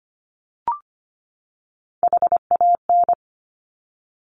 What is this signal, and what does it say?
Morse code, heard as a keyed tone, about 25 wpm, 700 Hz. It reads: HAN
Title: Top Three Letter Combinations in Words 25wpm